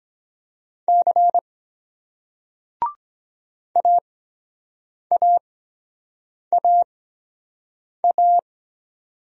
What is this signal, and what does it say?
Morse code, heard as a keyed tone, about 23 wpm, spaced as for 7 wpm, 700 Hz. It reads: C AAAA